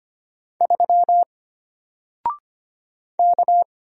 Morse code, heard as a keyed tone, about 25 wpm, 700 Hz. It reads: 3 K
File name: Single-Letter-Number-25wpm